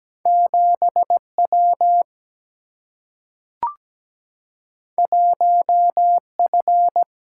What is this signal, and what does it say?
Morse code, heard as a keyed tone, about 17 wpm, 700 Hz. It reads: 7W 1F
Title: Single-Letter Single-Number 17wpm